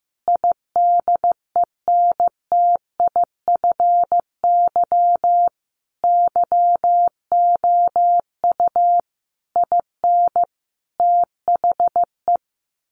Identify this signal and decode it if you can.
Morse code, heard as a keyed tone, about 15 wpm, 700 Hz. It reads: IDENTIFY YOU IN THE
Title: Sentences from Top 900 Words 15wpm